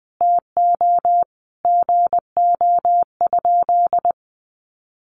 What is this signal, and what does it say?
Morse code, heard as a keyed tone, about 20 wpm, 700 Hz. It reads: TO GO?
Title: Sentences from Top 1300 Words 20wpm